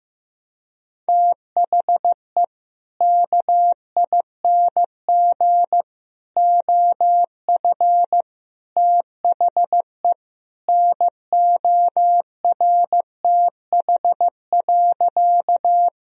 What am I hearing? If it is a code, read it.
Morse code, heard as a keyed tone, about 15 wpm, 700 Hz. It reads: THE KING OF THE NORTH.
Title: Sentences from Top 1000 Words - Review Part 1 - 15wpm